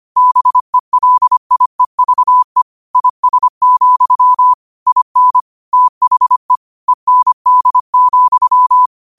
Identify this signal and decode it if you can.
Morse code, heard as a keyed tone, about 25 wpm, 1000 Hz. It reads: BELIEVE IS, IN THE END,
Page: New quotes every day in morse code at 25 Words per minute.